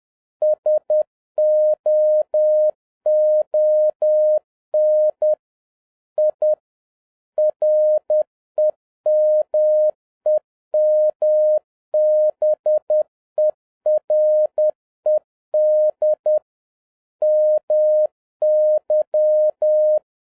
Morse code, heard as a keyed tone, about 10 wpm, 610 Hz. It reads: SOON I REMEMBERED MY